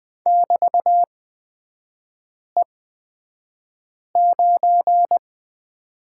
Morse code, heard as a keyed tone, about 20 wpm, spaced as for 5 wpm, 700 Hz. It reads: <BT>E9
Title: 224 Lesson 36e - BT - Three Characters 20wpm